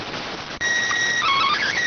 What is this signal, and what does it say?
SSTV (slow-scan television), PD 290.